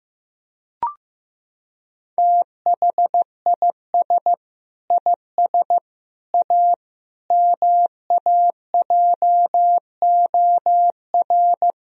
Morse code, heard as a keyed tone, about 16 wpm, 700 Hz. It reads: THIS IS A MAJOR
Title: Sentences from Top 1000 Words - Review Part 4 - 15wpm